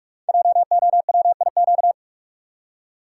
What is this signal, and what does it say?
Morse code, heard as a keyed tone, about 44 wpm, 690 Hz. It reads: JOWIQ
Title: Code Groups - 5 Letters - 45wpm